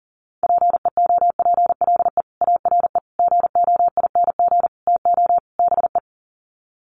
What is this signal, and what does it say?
Morse code, heard as a keyed tone, about 40 wpm, 700 Hz. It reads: PEOPLE ARE GOING TO BE
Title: Sets of 5 Words Encore 40wpm